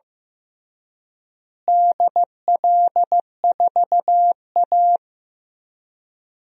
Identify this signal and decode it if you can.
Morse code, heard as a keyed tone, about 15 wpm, 700 Hz. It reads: DL4A
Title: DXCC-100 Call Signs - 4 Characters - Repeated 2x - 15wpm